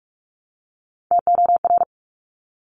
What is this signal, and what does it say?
Morse code, heard as a keyed tone, about 46 wpm, 700 Hz. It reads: TOR
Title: Top Three Letter Combinations in Words 45wpm